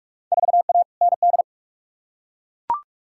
Morse code, heard as a keyed tone, about 42 wpm, 700 Hz. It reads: 4A ND
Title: Field Day Exchanges Encore - 45wpm